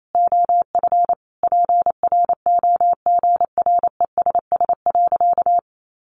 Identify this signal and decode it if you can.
Morse code, heard as a keyed tone, about 28 wpm, 700 Hz. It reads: OF PROGRESS.
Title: Sentences from Top 1400 Words 28wpm